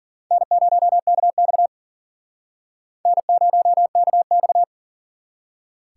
Morse code, heard as a keyed tone, about 52 wpm, 700 Hz. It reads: N0KX N0KX